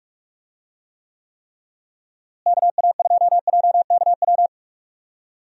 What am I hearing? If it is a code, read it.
Morse code, heard as a keyed tone, about 46 wpm, 700 Hz. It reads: KA2JKW
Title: USA Call Signs - 6 Characters - Repeated 3x - 45wpm